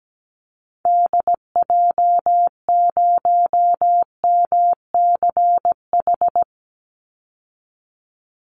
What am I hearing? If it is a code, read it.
Morse code, heard as a keyed tone, about 17 wpm, 700 Hz. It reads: DJ0MCH
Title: Call Signs Repeated 3x - 17wpm